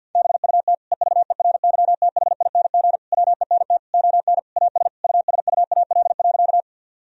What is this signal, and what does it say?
Morse code, heard as a keyed tone, about 52 wpm, 700 Hz. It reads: BUT EVERYTHING WENT ON AS USUAL.
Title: Sentences from Top 1000 Words - Review Part 2 - 50wpm